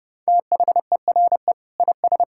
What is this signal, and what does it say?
Morse code, heard as a keyed tone, about 30 wpm, 700 Hz. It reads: THERE IS
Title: Sentences from Top 1000 Words - Review Part 5 - 30wpm